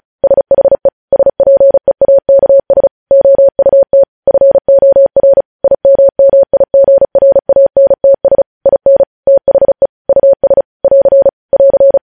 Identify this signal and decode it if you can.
Morse code, heard as a keyed tone, about 36 wpm, 550 Hz. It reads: SHE SPEAKS OUT FOR IMMIGRANTS IN THE US <AR> <AR>
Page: news headlines in morse code at 35 wpm